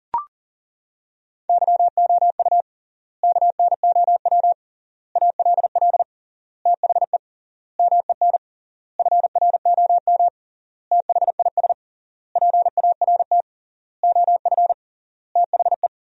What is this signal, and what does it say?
Morse code, heard as a keyed tone, about 37 wpm, 700 Hz. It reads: YOU KNOW ALL THE MEN FROM THIS PART OF THE
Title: 3x Word-Spacing - Sentences from Top 300 Words - 40wpm